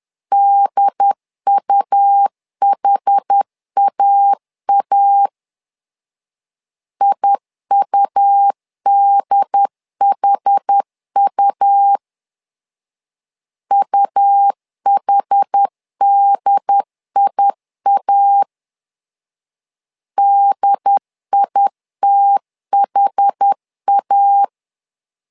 Morse code, heard as a keyed tone, about 10 wpm, 800 Hz. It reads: DUHAA IUDHU UHDIA DITHA
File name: LEZIONI / 2 UHDN / 2 UHD 10wpm 1826